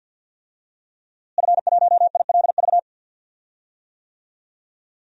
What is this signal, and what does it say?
Morse code, heard as a keyed tone, about 46 wpm, 700 Hz. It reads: F1ILV
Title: DXCC-100 Call Signs - 5 Characters - Repeated 3x - 50wpm